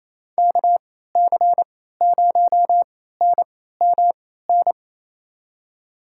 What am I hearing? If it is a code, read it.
Morse code, heard as a keyed tone, about 28 wpm, spaced as for 16 wpm, 700 Hz. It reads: KC0NMN
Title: USA Call Signs - 3x Character Spacing - 28wpm